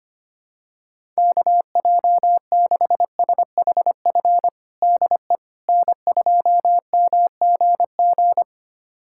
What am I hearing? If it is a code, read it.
Morse code, heard as a keyed tone, about 25 wpm, 700 Hz. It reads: KJ6SHF DE N2MGG
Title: USA Callsign DE Callsign 25wpm